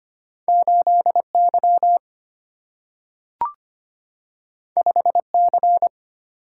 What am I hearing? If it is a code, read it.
Morse code, heard as a keyed tone, about 25 wpm, 700 Hz. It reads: 8 Y 5 C
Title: Single-Letter Single-Number 25wpm